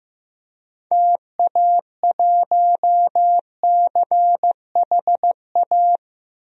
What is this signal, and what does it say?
Morse code, heard as a keyed tone, about 15 wpm, 700 Hz. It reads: TA1CHA
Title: DXCC-100 Call Signs - 6 Characters - 15wpm